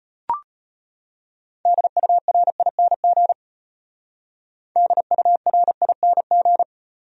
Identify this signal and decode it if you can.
Morse code, heard as a keyed tone, about 33 wpm, 700 Hz. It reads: DURING DURING